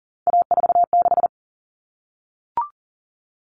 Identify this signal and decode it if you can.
Morse code, heard as a keyed tone, about 40 wpm, 700 Hz. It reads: A46
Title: Common Cut Numbers - 3-Digit 40wpm